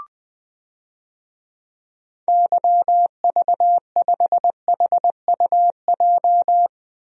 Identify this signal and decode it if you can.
Morse code, heard as a keyed tone, about 20 wpm, 700 Hz. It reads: YV5HUJ
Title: DXCC-100 Call Signs - 6 Characters - Repeated 2x - 20wpm